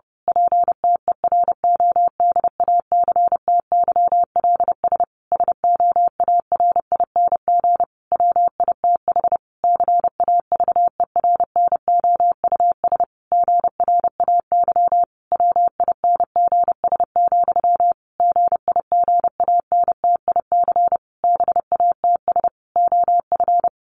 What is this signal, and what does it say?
Morse code, heard as a keyed tone, about 30 wpm, 700 Hz. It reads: PTERODACTYLS SOARING WITH CAVERNOUS GRAY WINGS, GIGANTIC BATS OF
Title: A Sound Of Thunder 30wpm